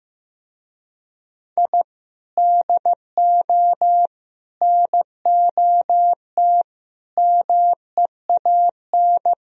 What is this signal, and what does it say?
Morse code, heard as a keyed tone, about 17 wpm, 700 Hz. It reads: I DO NOT MEAN
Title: Speed-Racing - Sets of 5 Words 15wpm